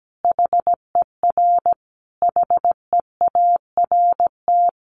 Morse code, heard as a keyed tone, about 17 wpm, 700 Hz. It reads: HER HEART
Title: Sentences from Top 1200 Words 17wpm